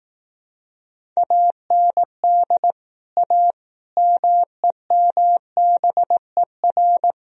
Morse code, heard as a keyed tone, about 18 wpm, 700 Hz. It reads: AND A MEMBER
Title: Speed-Racing - Sets of 5 Words 15wpm